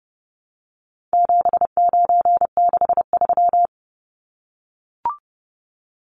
Morse code, heard as a keyed tone, about 30 wpm, 700 Hz. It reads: 7963
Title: Numbers - 4-Digit 30wpm